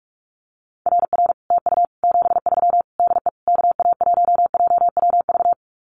Morse code, heard as a keyed tone, about 44 wpm, 700 Hz. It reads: RR TU 73 DE KA1JWV
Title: Common QSO Phrases Encore 45wpm